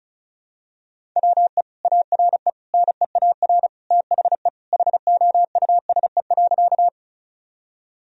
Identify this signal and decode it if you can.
Morse code, heard as a keyed tone, about 36 wpm, 700 Hz. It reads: WE ARE NEAR THE HOUSE.
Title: Sentences from Top 1000 Words - Review Part 1 - 35wpm